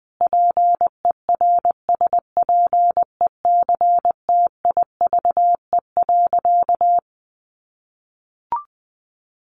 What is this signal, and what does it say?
Morse code, heard as a keyed tone, about 20 wpm, 700 Hz. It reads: PERSPECTIVE.
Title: Sentences from Top 1200 Words 20wpm